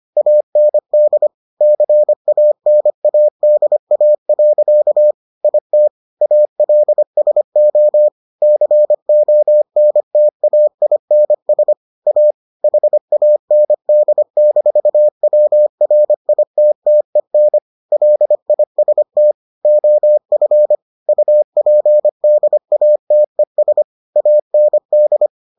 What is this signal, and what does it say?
Morse code, heard as a keyed tone, about 25 wpm, 600 Hz. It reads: AND CANADA. IT ALSO CONTAINS A HAND-WRITTEN LIST OF UPDATES AND